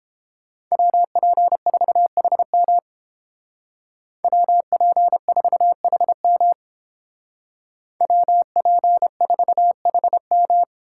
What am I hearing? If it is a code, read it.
Morse code, heard as a keyed tone, about 30 wpm, 700 Hz. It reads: WP4HM WP4HM WP4HM